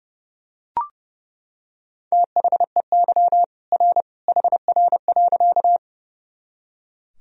Morse code, heard as a keyed tone, about 30 wpm, 700 Hz. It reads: THEY R HR.
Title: Sentences from Top 200 Words and Abbreviations 30wpm